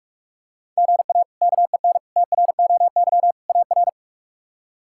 Morse code, heard as a keyed tone, about 46 wpm, 700 Hz. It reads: GA KEN TROY AR
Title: K1USN SST QSO Practice - 45wpm